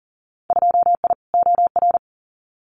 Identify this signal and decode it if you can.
Morse code, heard as a keyed tone, about 40 wpm, 700 Hz. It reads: 2I OR
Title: Winter Field Day Exchanges Encore 40wpm